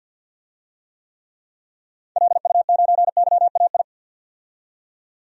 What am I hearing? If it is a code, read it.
Morse code, heard as a keyed tone, about 52 wpm, 700 Hz. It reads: LU9YAI
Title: DXCC-100 Call Signs - 6 Characters - 50wpm